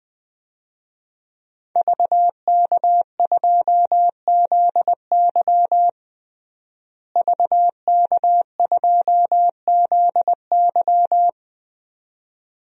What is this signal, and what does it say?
Morse code, heard as a keyed tone, about 20 wpm, 700 Hz. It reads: VK2ZY VK2ZY
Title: DXCC-100 Call Signs - 5 Characters - Repeated 2x - 20wpm